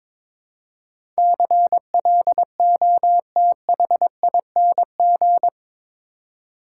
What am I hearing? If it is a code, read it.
Morse code, heard as a keyed tone, about 22 wpm, 700 Hz. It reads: CLOTHING
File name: practice / 22wpm / Eight Letter Words - 22wpm